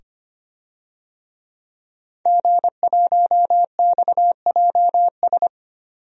Morse code, heard as a keyed tone, about 25 wpm, 700 Hz. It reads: G1XJS
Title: DXCC-100 Call Signs - 5 Characters - 25wpm